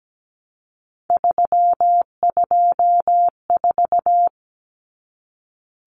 Morse code, heard as a keyed tone, about 17 wpm, 700 Hz. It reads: 324
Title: Numbers - 3-Digit 17wpm